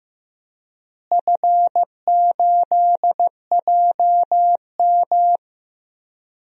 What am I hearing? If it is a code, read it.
Morse code, heard as a keyed tone, about 15 wpm, 700 Hz. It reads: F8JM
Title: DXCC-100 Call Signs - 4 Characters - Repeated 2x - 15wpm